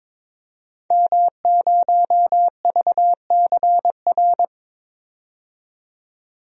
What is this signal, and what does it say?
Morse code, heard as a keyed tone, about 22 wpm, 700 Hz. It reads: M0VCR
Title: DXCC-100 Call Signs - 5 Characters - 22wpm